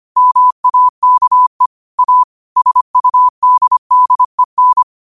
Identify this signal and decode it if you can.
Morse code, heard as a keyed tone, about 25 wpm, 1000 Hz. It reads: MAKE A SUDDEN